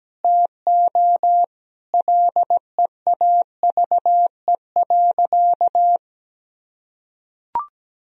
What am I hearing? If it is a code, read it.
Morse code, heard as a keyed tone, about 17 wpm, 700 Hz. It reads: TO LEAVE.
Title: Sentences from Top 1000 Words - Review Part 4 - 17wpm